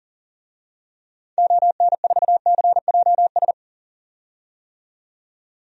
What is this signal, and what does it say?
Morse code, heard as a keyed tone, about 40 wpm, 700 Hz. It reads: ON4CJS
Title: DXCC-100 Call Signs - 6 Characters - 40wpm